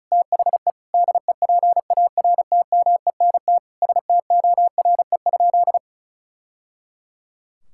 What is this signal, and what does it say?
Morse code, heard as a keyed tone, about 36 wpm, 700 Hz. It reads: THE DEPARTMENT STORE?
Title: Sentences from Top 1000 Words - Review Part 5 - 35wpm